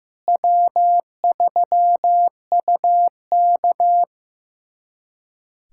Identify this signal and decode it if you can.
Morse code, heard as a keyed tone, about 15 wpm, 700 Hz. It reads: W3UK
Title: USA Callsign DE Callsign - 4 Characters - 15wpm